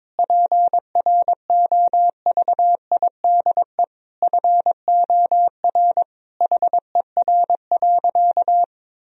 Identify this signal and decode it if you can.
Morse code, heard as a keyed tone, about 22 wpm, 700 Hz. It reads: PROVIDE FOR HER.
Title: Sentences from Top 1000 Words - Review Part 3 - 22wpm